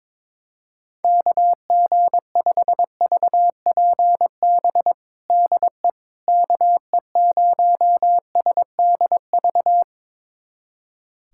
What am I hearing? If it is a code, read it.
Morse code, heard as a keyed tone, about 22 wpm, 700 Hz. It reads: KG5VPB DE KE0SDV
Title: USA Callsign DE Callsign 22wpm